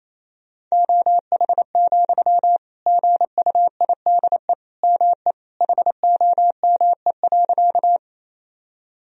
Morse code, heard as a keyed tone, about 28 wpm, 700 Hz. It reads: OH, GUIDE ME HOME.
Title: Sentences from Top 1000 Words - Review Part 3 - 28wpm